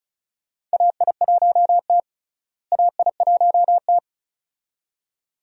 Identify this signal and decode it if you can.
Morse code, heard as a keyed tone, about 36 wpm, 700 Hz. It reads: AI1T AI1T
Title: USA Call Signs - 4 Characters - Repeated 2x - 35wpm